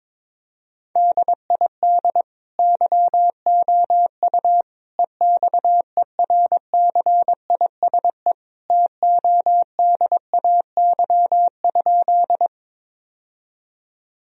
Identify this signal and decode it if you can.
Morse code, heard as a keyed tone, about 21 wpm, 700 Hz. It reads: DID YOU EXERCISE TODAY?
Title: Sentences from Top 1000 Words - Review Part 4 - 22wpm